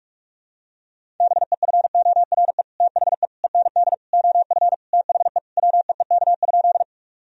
Morse code, heard as a keyed tone, about 41 wpm, 700 Hz. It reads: BEFORE THE END OF THE WEEK?
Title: Sentences from Top 1000 Words - Review Part 2 - 45wpm